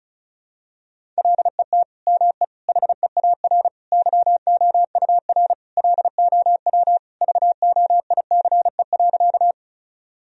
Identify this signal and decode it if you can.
Morse code, heard as a keyed tone, about 36 wpm, 700 Hz. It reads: LET ME HEAR YOUR LOW VOICE.
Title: Sentences from Top 1000 Words - Review Part 1 - 35wpm